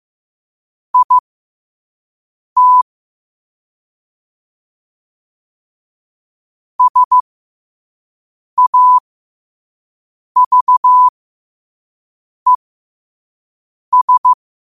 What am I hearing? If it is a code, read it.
Morse code, heard as a keyed tone, about 15 wpm, spaced as for 5 wpm, 1000 Hz. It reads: IT SAVES